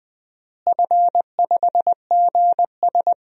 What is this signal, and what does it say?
Morse code, heard as a keyed tone, about 20 wpm, 700 Hz. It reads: F5GS
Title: DXCC-100 Call Signs - 4 Characters - Repeated 2x - 20wpm